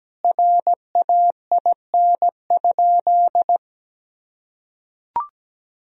Morse code, heard as a keyed tone, about 17 wpm, 700 Hz. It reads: RAIN?
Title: Sentences from Top 1000 Words - Review Part 1 - 17wpm